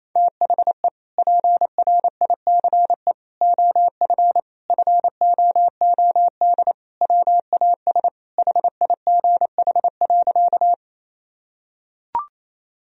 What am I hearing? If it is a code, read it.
Morse code, heard as a keyed tone, about 28 wpm, 700 Hz. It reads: THE PRICE OF FOOD WAS HIGH.
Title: Sentences from Top 1000 Words - Review Part 4 - 28wpm